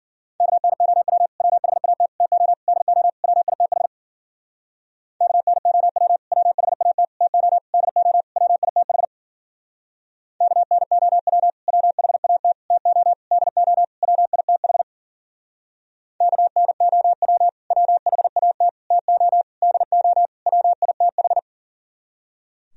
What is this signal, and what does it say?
Morse code, heard as a keyed tone, about 51 wpm, 700 Hz. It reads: KNOW WHAT TO DO WITH KNOW WHAT TO DO WITH KNOW WHAT TO DO WITH KNOW WHAT TO DO WITH